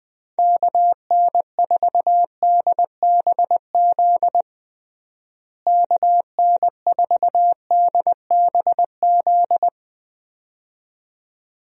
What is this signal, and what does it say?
Morse code, heard as a keyed tone, about 20 wpm, 700 Hz. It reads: KN4DBZ KN4DBZ
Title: USA Call Signs - 6 Characters - Repeated 2x - 20wpm